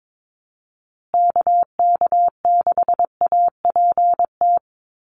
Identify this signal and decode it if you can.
Morse code, heard as a keyed tone, about 22 wpm, 700 Hz. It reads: KK6APT
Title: USA Call Signs Repeated 2x 22wpm